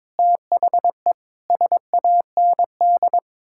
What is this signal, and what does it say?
Morse code, heard as a keyed tone, about 22 wpm, 700 Hz. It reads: THE SAND
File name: practice / 22wpm / Sets of 7 Words - 22wpm